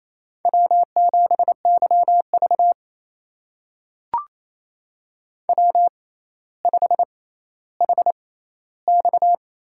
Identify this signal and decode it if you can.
Morse code, heard as a keyed tone, about 28 wpm, spaced as for 14 wpm, 700 Hz. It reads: W7YV W5HX
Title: USA Call Signs - 4 Characters - 6x Character Spacing - 28wpm